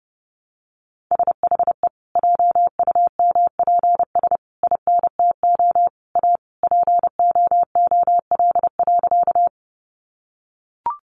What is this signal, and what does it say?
Morse code, heard as a keyed tone, about 30 wpm, 700 Hz. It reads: SHE JUMPS INTO A POOL.
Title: Additional Easy Phrases 30wpm